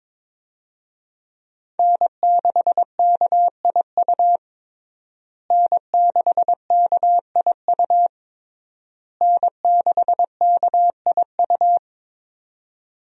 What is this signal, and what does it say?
Morse code, heard as a keyed tone, about 22 wpm, 700 Hz. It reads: N6KIU N6KIU N6KIU
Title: USA Call Signs - 5 Characters - Repeated 3x - 22wpm